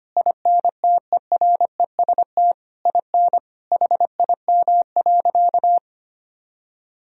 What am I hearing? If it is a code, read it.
Morse code, heard as a keyed tone, about 25 wpm, 700 Hz. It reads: INTEREST IN HIM.
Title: Sentences from Top 1000 Words - Review Part 3 - 25wpm